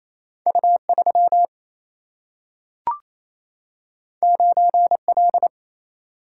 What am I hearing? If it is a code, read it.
Morse code, heard as a keyed tone, about 28 wpm, 700 Hz. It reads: U3 9L
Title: Single-Letter Single-Number 28wpm